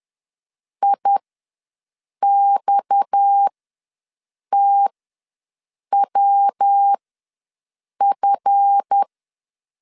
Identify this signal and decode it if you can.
Morse code, heard as a keyed tone, about 10 wpm, spaced as for 6 wpm, 800 Hz. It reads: IXTWF
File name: LEZIONI / 9 28 ws 10wpm 4124